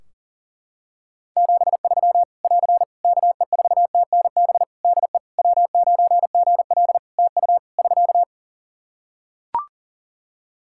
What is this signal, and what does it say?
Morse code, heard as a keyed tone, about 40 wpm, 700 Hz. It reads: 73 <AR> KE4TNB DE W9GL TU <SK>
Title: K1USN SST QSO Practice 40wpm